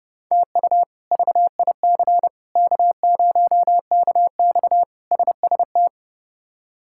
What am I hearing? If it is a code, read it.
Morse code, heard as a keyed tone, about 32 wpm, 700 Hz. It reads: TU VIC K0KX SST
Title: K1USN SST QSO Practice - 30wpm